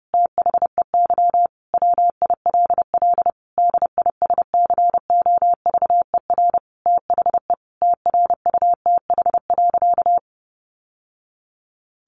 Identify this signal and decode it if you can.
Morse code, heard as a keyed tone, about 28 wpm, 700 Hz. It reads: THEY WILL DISCOVER THE TRUTH.
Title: Sentences from Top 1100 Words 30wpm